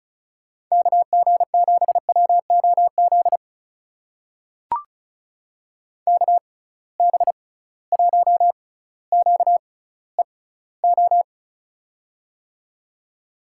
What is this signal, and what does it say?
Morse code, heard as a keyed tone, about 34 wpm, spaced as for 18 wpm, 700 Hz. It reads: KG7WOZ KB1QEO
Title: USA Call Signs - 6 Characters - 6x Character Spacing - 35wpm